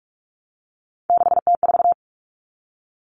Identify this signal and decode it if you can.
Morse code, heard as a keyed tone, about 44 wpm, 700 Hz. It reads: 6T4
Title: Common Cut Numbers - 3-Digit 45wpm